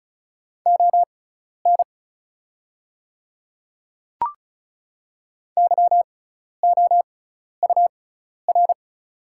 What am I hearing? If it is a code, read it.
Morse code, heard as a keyed tone, about 36 wpm, spaced as for 12 wpm, 700 Hz. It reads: ON YOUR
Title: 6x Character Spacing - Top 100 Words - 35wpm